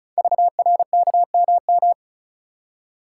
Morse code, heard as a keyed tone, about 36 wpm, 690 Hz. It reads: VRKMM
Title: Code Groups - 5 Letters - 35wpm